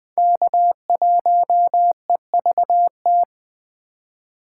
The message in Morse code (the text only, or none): K1EVT